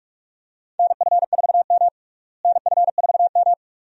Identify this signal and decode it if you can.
Morse code, heard as a keyed tone, about 48 wpm, 700 Hz. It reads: NF4M NF4M
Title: USA Call Signs - 4 Characters - Repeated 2x - 45wpm